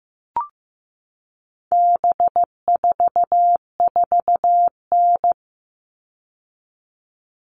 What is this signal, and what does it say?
Morse code, heard as a keyed tone, about 15 wpm, 700 Hz. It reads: B44N
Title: All Cut Numbers - 4-Digit 15wpm